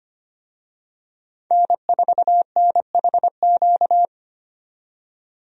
Morse code, heard as a keyed tone, about 25 wpm, 700 Hz. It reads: N4NHQ
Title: USA Call Signs - 5 Characters - 25wpm